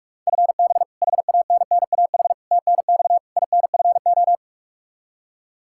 Morse code, heard as a keyed tone, about 41 wpm, 700 Hz. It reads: FB HANNAH TNX INFO
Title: Practice QSO - Exchange 3 - More Info - 45wpm